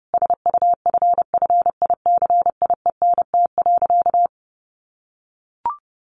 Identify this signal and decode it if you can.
Morse code, heard as a keyed tone, about 30 wpm, 700 Hz. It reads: SUFFICIENT.
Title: Sentences from Top 1400 Words 30wpm